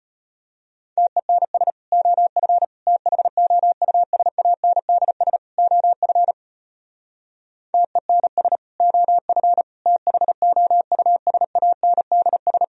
Morse code, heard as a keyed tone, about 32 wpm, 700 Hz. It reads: TENS OF THOUSANDS OF TENS OF THOUSANDS